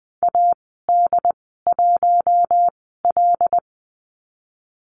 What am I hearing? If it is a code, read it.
Morse code, heard as a keyed tone, about 20 wpm, spaced as for 15 wpm, 700 Hz. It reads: AD1L
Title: 082 Lesson 14g - Number 2 - Call Signs 20wpm